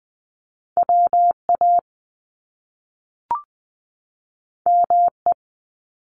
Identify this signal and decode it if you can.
Morse code, heard as a keyed tone, about 20 wpm, 700 Hz. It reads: WA ME
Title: Top Two Letter Combinations in Words 20wpm